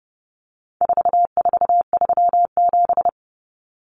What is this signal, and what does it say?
Morse code, heard as a keyed tone, about 30 wpm, 700 Hz. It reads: 4437
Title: Numbers - 4-Digit 30wpm